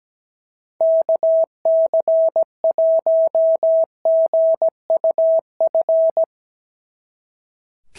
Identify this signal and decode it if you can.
Morse code, heard as a keyed tone, about 16 wpm, 650 Hz. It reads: KC1GUF
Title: SOTA QSO Practice 17wpm